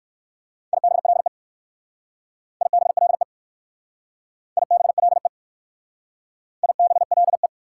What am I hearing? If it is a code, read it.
Morse code, heard as a keyed tone, about 54 wpm, 690 Hz. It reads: IBLE IBLE IBLE IBLE
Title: Speed-Racing Prefixes and Suffixes 45wpm